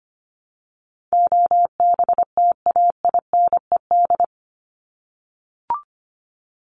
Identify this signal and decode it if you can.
Morse code, heard as a keyed tone, about 25 wpm, 700 Hz. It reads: OBTAINED
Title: ED Suffix - Top 100 Words with Base Words 25wpm